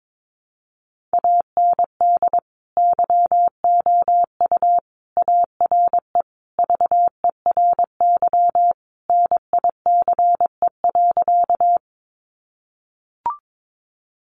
Morse code, heard as a keyed tone, about 22 wpm, 700 Hz. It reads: AND YOU ARE VERY NICE.
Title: Sentences from Top 1100 Words 22wpm